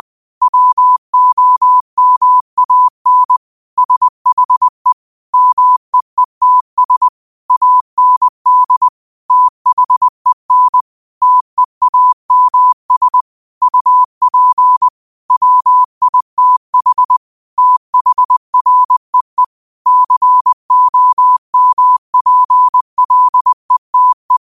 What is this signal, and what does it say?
Morse code, heard as a keyed tone, about 20 wpm, 1000 Hz. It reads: WOMAN SHE MEETS AND THEN TEAMS UP WITH THREE COMPLETE
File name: qotdfiles / 20wpm